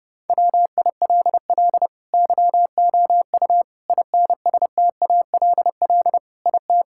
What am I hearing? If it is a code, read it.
Morse code, heard as a keyed tone, about 30 wpm, 700 Hz. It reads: WILL YOU INSTALL IT
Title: Sentences from Top 1600 Words 30wpm